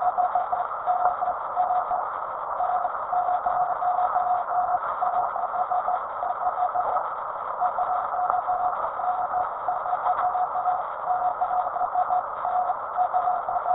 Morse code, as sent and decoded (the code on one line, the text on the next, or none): none